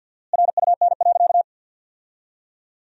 Morse code, heard as a keyed tone, about 52 wpm, 700 Hz. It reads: RUN.
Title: 4x Word-Spacing - Sentences from Top 500 Words - 50wpm